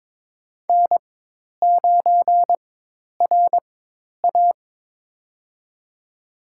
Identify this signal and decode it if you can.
Morse code, heard as a keyed tone, about 21 wpm, spaced as for 10 wpm, 700 Hz. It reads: N9RA
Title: USA Call Signs - 4x Character Spacing - 22wpm